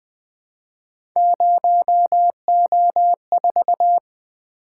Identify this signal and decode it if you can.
Morse code, heard as a keyed tone, about 20 wpm, 700 Hz. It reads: 0O4
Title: Three Letter-Number Combinations - 20wpm